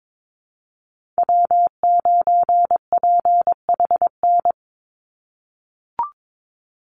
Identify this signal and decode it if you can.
Morse code, heard as a keyed tone, about 22 wpm, 700 Hz. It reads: W9PHN
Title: USA Call Signs 22wpm